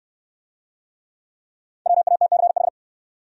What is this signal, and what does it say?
Morse code, heard as a keyed tone, about 69 wpm, 700 Hz. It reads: WATCH